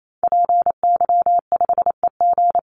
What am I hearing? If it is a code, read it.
Morse code, heard as a keyed tone, about 28 wpm, 700 Hz. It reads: PY5EG
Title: Call Signs - 28wpm